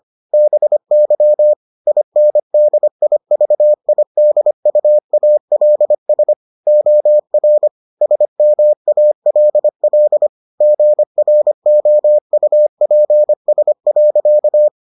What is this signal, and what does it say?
Morse code, heard as a keyed tone, about 25 wpm, 600 Hz. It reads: BY INDIVIDUALS OR SMALL GROUPS.